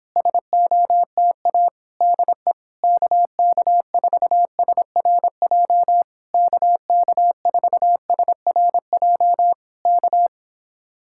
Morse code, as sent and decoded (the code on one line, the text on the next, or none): ... --- - .- / -.. / -.- -.- ....- ... .-. .--- / -.- -.- ....- ... .-. .--- / -.-
SOTA DE KK4SRJ KK4SRJ K